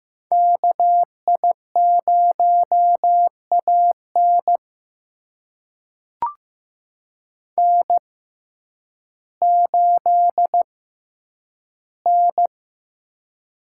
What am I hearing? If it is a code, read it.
Morse code, heard as a keyed tone, about 15 wpm, spaced as for 9 wpm, 700 Hz. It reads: KI0AN N8N
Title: USA Call Signs - 5 Characters - 6x Character Spacing - 15wpm